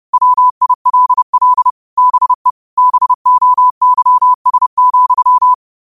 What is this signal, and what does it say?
Morse code, heard as a keyed tone, about 30 wpm, 1000 Hz. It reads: WILL BE BOYS,
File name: qotdfiles / 30wpm